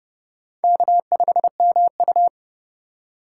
Morse code, heard as a keyed tone, about 29 wpm, 700 Hz. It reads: K5MU